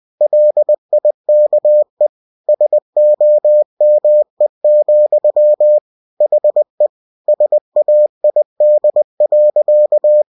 Morse code, heard as a keyed tone, about 20 wpm, 600 Hz. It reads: LIKE SOME, HE SAID.